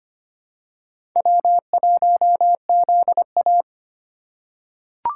Morse code, heard as a keyed tone, about 25 wpm, 700 Hz. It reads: W1ZA
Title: USA Call Signs - 3x Character Spacing - 25wpm